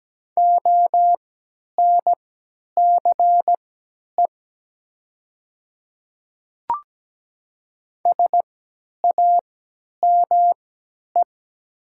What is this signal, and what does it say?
Morse code, heard as a keyed tone, about 17 wpm, spaced as for 10 wpm, 700 Hz. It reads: ONCE SAME
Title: 3x Character Spacing - Top 500 Words - 17wpm